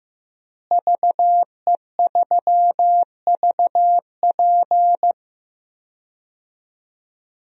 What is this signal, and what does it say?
Morse code, heard as a keyed tone, about 15 wpm, 700 Hz. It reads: VE3VP
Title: DXCC-100 Call Signs - 5 Characters - Repeated 2x - 15wpm